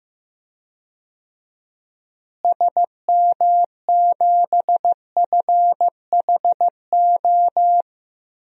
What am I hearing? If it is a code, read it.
Morse code, heard as a keyed tone, about 15 wpm, 700 Hz. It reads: SM7FHO
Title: DXCC-100 Call Signs - 6 Characters - Repeated 3x - 15wpm